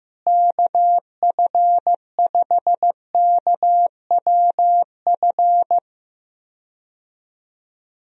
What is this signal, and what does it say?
Morse code, heard as a keyed tone, about 15 wpm, 700 Hz. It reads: KF5KWF
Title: USA Call Signs - 6 Characters - Repeated 2x - 15wpm